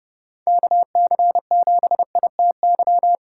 Morse code, heard as a keyed tone, about 40 wpm, 700 Hz. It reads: KC7ITY